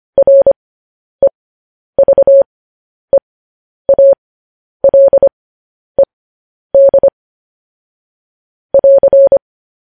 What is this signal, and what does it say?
Morse code, heard as a keyed tone, about 25 wpm, spaced as for 10 wpm, 550 Hz. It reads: REVEALED <AR>